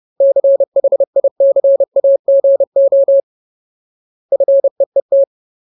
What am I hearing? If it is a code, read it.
Morse code, heard as a keyed tone, about 30 wpm, 550 Hz. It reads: CHICAGO FEET
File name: Ultimate Word List / 30WPM Ultimate